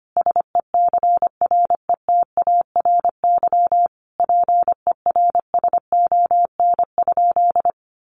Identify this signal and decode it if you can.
Morse code, heard as a keyed tone, about 25 wpm, 700 Hz. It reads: SECRETARY PERSON?
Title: Sentences from Top 1300 Words 25wpm